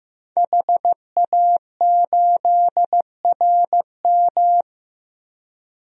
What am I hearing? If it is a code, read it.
Morse code, heard as a keyed tone, about 15 wpm, 700 Hz. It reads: HA8RM
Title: DXCC-100 Call Signs - 5 Characters - Repeated 2x - 15wpm